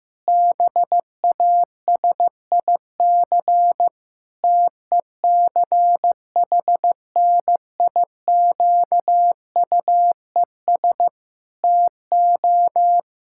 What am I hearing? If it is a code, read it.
Morse code, heard as a keyed tone, about 15 wpm, 700 Hz. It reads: BASIC TECHNIQUES TO
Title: Pangram Challenge v2 - 15wpm